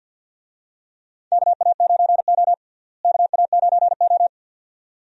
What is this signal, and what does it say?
Morse code, heard as a keyed tone, about 52 wpm, 700 Hz. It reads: KA9O KA9O
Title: USA Call Signs - 4 Characters - Repeated 2x - 50wpm